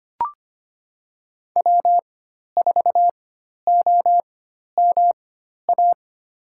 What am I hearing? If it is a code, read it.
Morse code, heard as a keyed tone, about 25 wpm, spaced as for 12 wpm, 700 Hz. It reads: W4OMA
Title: USA Call Signs - 4x Character Spacing - 25wpm